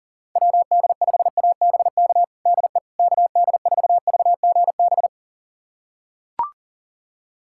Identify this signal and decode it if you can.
Morse code, heard as a keyed tone, about 37 wpm, 700 Hz. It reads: WD5ABK DE KD4VGB
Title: USA Callsign DE Callsign 40wpm